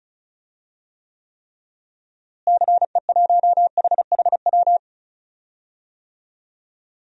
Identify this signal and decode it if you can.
Morse code, heard as a keyed tone, about 36 wpm, 700 Hz. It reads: CE1HHW
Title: DXCC-100 Call Signs - 6 Characters - Repeated 3x - 35wpm